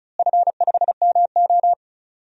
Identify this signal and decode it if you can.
Morse code, heard as a keyed tone, about 36 wpm, 700 Hz. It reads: F5MO
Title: DXCC-100 Call Signs - 4 Characters - Repeated 2x - 35wpm